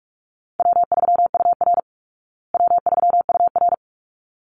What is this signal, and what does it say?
Morse code, heard as a keyed tone, about 44 wpm, 700 Hz. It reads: W3UR W3UR
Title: Easy Call Signs Repeated 2x - 45wpm